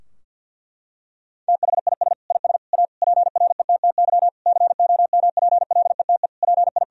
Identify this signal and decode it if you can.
Morse code, heard as a keyed tone, about 52 wpm, 700 Hz. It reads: THIS IS A PRETTY COMPLETE PI
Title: Sentences from Top 1000 Words - Review Part 2 - 50wpm